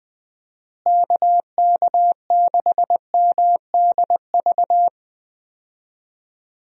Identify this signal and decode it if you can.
Morse code, heard as a keyed tone, about 20 wpm, 700 Hz. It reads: KK6MDV
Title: USA Call Signs - 6 Characters - Repeated 3x - 20wpm